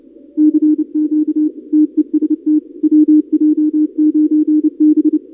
Morse code, heard as a keyed tone, about 29 wpm, 300 Hz. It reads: CQ TEST WJ9B
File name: contests / 2019-cw-wpx-kt5j-k5pi- -2330 0-31